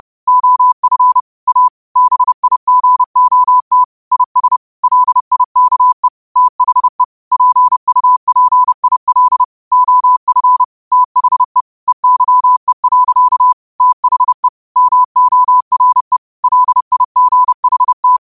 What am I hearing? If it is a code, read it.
Morse code, heard as a keyed tone, about 30 wpm, 1000 Hz. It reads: OF A BIGOT IS LIKE THE PUPIL OF THE EYE. THE MORE LIGHT